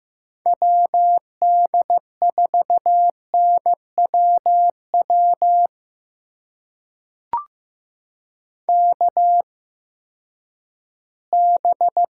Morse code, heard as a keyed tone, about 15 wpm, spaced as for 10 wpm, 700 Hz. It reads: WD4NWW KB
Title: USA Call Signs - 6 Characters - 8x Character Spacing - 15wpm